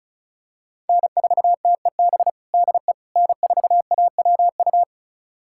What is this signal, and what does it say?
Morse code, heard as a keyed tone, about 36 wpm, 700 Hz. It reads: N4TEB DE N4AWU
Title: USA Callsign DE Callsign - 5 Characters - 35wpm